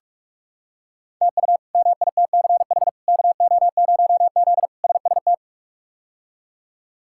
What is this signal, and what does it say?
Morse code, heard as a keyed tone, about 46 wpm, 700 Hz. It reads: TU MITCH KO0Z SST
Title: K1USN SST QSO Practice - 45wpm